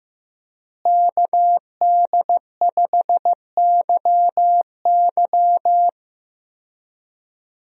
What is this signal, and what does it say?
Morse code, heard as a keyed tone, about 16 wpm, 700 Hz. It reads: KD5YY
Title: USA Call Signs - 5 Characters - Repeated 2x - 15wpm